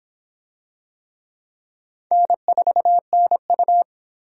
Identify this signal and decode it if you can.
Morse code, heard as a keyed tone, about 25 wpm, 700 Hz. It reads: N4NU